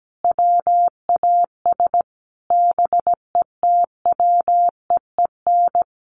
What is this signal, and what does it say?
Morse code, heard as a keyed tone, about 17 wpm, 700 Hz. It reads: WAS BETWEEN
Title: Sentences from Top 1300 Words 17wpm